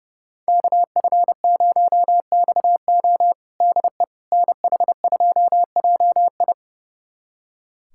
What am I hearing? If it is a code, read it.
Morse code, heard as a keyed tone, about 30 wpm, 700 Hz. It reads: KF0XO DE NH2JI
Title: USA Callsign DE Callsign - 5 Characters - 30wpm